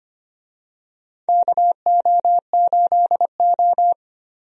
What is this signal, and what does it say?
Morse code, heard as a keyed tone, about 25 wpm, 700 Hz. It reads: KO8O
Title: USA Call Signs - 4 Characters - Repeated 2x - 25wpm